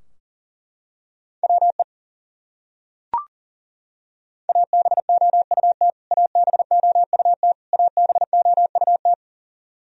Morse code, heard as a keyed tone, about 40 wpm, 700 Hz. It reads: WE ABOUT ABOUT ABOUT
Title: Top 100 Words Repeated 40wpm